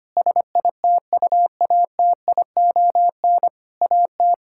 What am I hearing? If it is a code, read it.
Morse code, heard as a keyed tone, about 25 wpm, 700 Hz. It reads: SITUATION AT
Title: Sentences from Top 1600 Words 25wpm